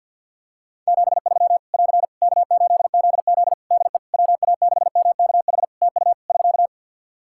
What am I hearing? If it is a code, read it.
Morse code, heard as a keyed tone, about 46 wpm, 700 Hz. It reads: 73 <AR> K8ZZ DE WA6MKH TU <SK>
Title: K1USN SST QSO Practice 50wpm